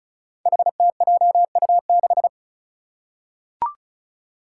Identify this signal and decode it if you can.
Morse code, heard as a keyed tone, about 36 wpm, 700 Hz. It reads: HTJU6
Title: Five Challenging Letter-Number Combinations 35wpm